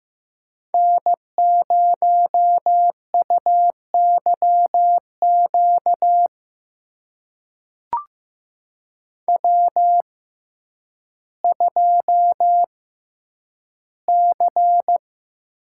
Morse code, heard as a keyed tone, about 15 wpm, spaced as for 9 wpm, 700 Hz. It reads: N0UYQ W2C
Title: USA Call Signs - 5 Characters - 6x Character Spacing - 15wpm